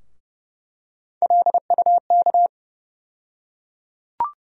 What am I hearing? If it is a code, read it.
Morse code, heard as a keyed tone, about 30 wpm, 700 Hz. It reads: LUK
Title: USA Call Sign Postfixes - 3 Letters 30wpm